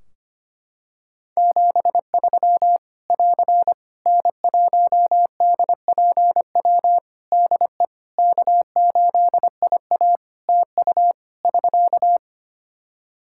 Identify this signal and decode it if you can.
Morse code, heard as a keyed tone, about 25 wpm, 700 Hz. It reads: 73 <AR> N1DPW DE K8IA TU <SK>
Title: K1USN SST QSO Practice 25wpm